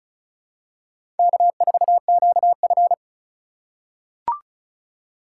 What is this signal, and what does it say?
Morse code, heard as a keyed tone, about 34 wpm, 700 Hz. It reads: K4QF
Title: USA Call Signs - 4 Characters - 8x Character Spacing - 35wpm